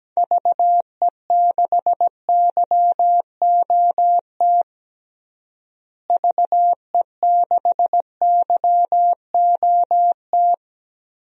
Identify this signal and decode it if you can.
Morse code, heard as a keyed tone, about 17 wpm, 700 Hz. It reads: VE6YOT VE6YOT
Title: DXCC-100 Call Signs - 6 Characters - Repeated 3x - 17wpm